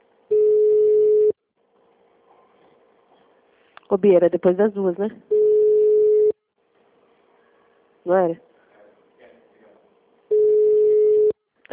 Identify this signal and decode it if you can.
Ringback tone (Europe, 425 Hz), with speech.